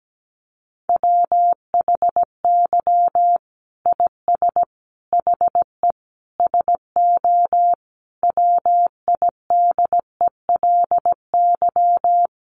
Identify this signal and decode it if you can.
Morse code, heard as a keyed tone, about 17 wpm, 700 Hz. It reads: WHY IS HE SO WIDELY
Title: Sentences from Top 1200 Words 17wpm